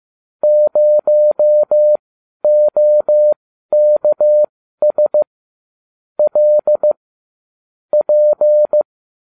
Morse code, heard as a keyed tone, about 15 wpm, spaced as for 12 wpm, 600 Hz. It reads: 0OKS L P